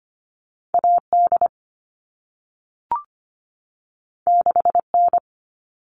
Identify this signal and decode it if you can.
Morse code, heard as a keyed tone, about 25 wpm, 700 Hz. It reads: A D 6 N
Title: All Cut Numbers - 2-Digit 25wpm